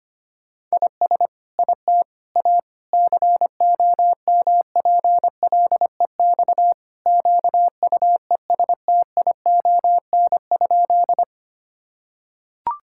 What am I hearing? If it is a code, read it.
Morse code, heard as a keyed tone, about 25 wpm, 700 Hz. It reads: IS IT A COMPLEX QUESTION?
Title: Sentences from Top 1000 Words - Review Part 6 - 25wpm